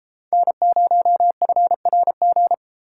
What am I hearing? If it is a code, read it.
Morse code, heard as a keyed tone, about 32 wpm, 700 Hz. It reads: N0FRG